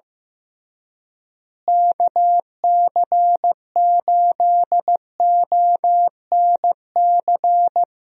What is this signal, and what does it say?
Morse code, heard as a keyed tone, about 15 wpm, 700 Hz. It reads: KC8ONC
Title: USA Call Signs - 6 Characters - Repeated 3x - 15wpm